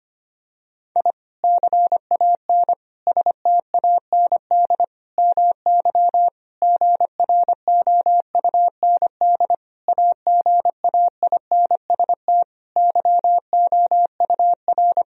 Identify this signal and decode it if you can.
Morse code, heard as a keyed tone, about 25 wpm, 700 Hz. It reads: I CAN STAND MY GROUND AGAINST YOUR
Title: Sentences from Top 1000 Words - Review Part 2 - 25wpm